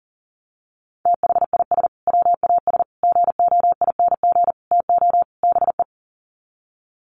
Morse code, heard as a keyed tone, about 40 wpm, 700 Hz. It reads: THIS WAS GOING TO BE
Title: Sets of 5 Words Encore2 40wpm